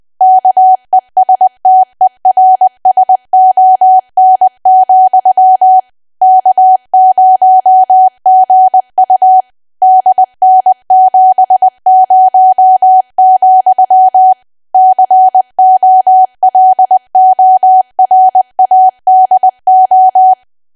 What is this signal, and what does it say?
Morse code, heard as a keyed tone, about 20 wpm, 750 Hz. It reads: KESTERSON, K0GU DN70, COLORADO
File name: cwa / W1AW-Level3 / 20WPM